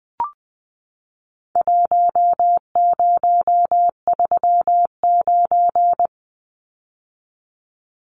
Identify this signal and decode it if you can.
Morse code, heard as a keyed tone, about 20 wpm, 700 Hz. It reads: 1039
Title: Numbers - Contest Style - 4-Digit 20wpm